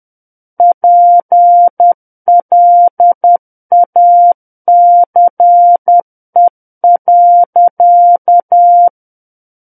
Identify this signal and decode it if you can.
Morse code, heard as a keyed tone, about 10 wpm, 700 Hz. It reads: PLACE.
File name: Book One / 01-Chapter 1 - 10 WPM